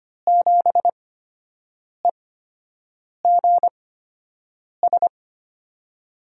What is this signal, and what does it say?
Morse code, heard as a keyed tone, about 25 wpm, spaced as for 7 wpm, 700 Hz. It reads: 7EGS
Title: USA Call Signs - 6 Characters - 8x Character Spacing - 25wpm